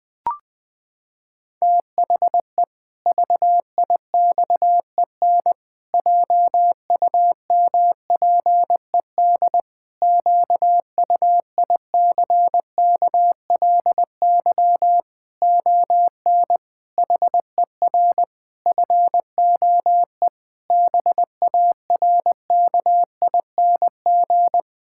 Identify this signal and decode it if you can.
Morse code, heard as a keyed tone, about 20 wpm, 700 Hz. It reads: THE VIXEN JUMPED QUICKLY ON HER FOE BARKING
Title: Pangram Challenge v2 - 20wpm